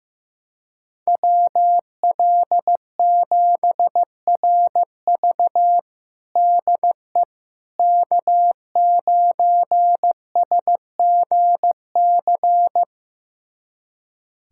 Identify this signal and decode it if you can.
Morse code, heard as a keyed tone, about 15 wpm, 700 Hz. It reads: WL7RV DE K9SGC
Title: USA Callsign DE Callsign - 5 Characters - 15wpm